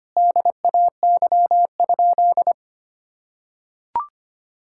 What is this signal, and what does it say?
Morse code, heard as a keyed tone, about 25 wpm, 700 Hz. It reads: DAY?
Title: Sentences from Top 1000 Words - Review Part 5 - 25wpm